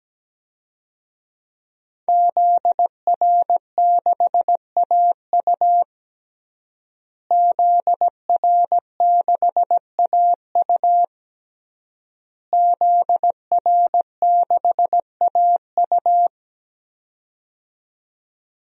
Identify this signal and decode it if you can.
Morse code, heard as a keyed tone, about 17 wpm, 700 Hz. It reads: ZR6AU ZR6AU ZR6AU
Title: DXCC-100 Call Signs - 5 Characters - Repeated 3x - 17wpm